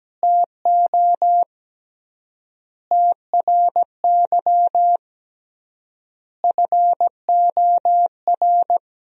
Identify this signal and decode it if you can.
Morse code, heard as a keyed tone, about 17 wpm, 700 Hz. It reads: TO TRY FOR
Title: 3x Word-Spacing - Sentences from Top 400 Words - 17wpm